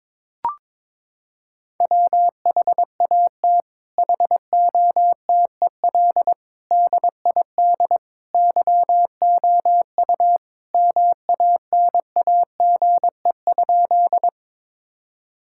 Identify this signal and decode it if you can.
Morse code, heard as a keyed tone, about 21 wpm, 700 Hz. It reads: WHAT HOTEL DID YOU MANAGE?
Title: Sentences from Top 1600 Words 22wpm